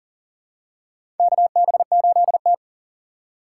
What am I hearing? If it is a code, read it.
Morse code, heard as a keyed tone, about 40 wpm, 700 Hz. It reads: KB8T
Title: USA Call Signs - 4 Characters - 8x Character Spacing - 40wpm